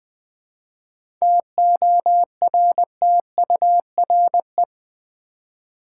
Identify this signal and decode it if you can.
Morse code, heard as a keyed tone, about 20 wpm, 700 Hz. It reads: TORTURE
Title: 7 Letter Words - First 3 Letters Given - 20wpm